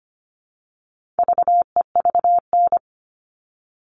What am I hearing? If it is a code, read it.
Morse code, heard as a keyed tone, about 25 wpm, 700 Hz. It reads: VEVN
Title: All Cut Numbers - 4-Digit 25wpm